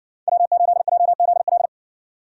Morse code, heard as a keyed tone, about 53 wpm, 700 Hz. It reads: W8JZL